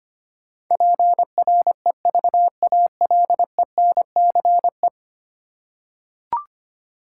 Morse code, heard as a keyed tone, about 25 wpm, 700 Hz. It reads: PREVALENCE
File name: 10 Letter Words - First 3 Letters Given - 25wpm